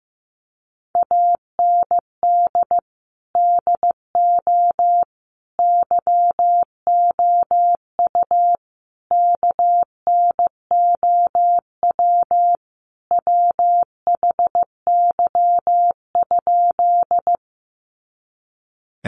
Morse code, heard as a keyed tone, about 15 wpm, 700 Hz. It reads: AND DO YOU KNOW WHY?
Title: Sentences from Top 200 Words 15wpm